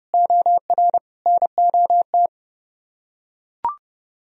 Morse code, heard as a keyed tone, about 30 wpm, 700 Hz. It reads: OR NOT